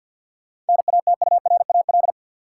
Morse code, heard as a keyed tone, about 52 wpm, 700 Hz. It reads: NATURAL
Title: Sentences from Top 700 Words - New Word Drill - 50wpm